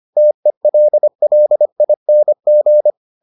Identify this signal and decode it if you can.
Morse code, heard as a keyed tone, about 25 wpm, 600 Hz. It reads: TELLING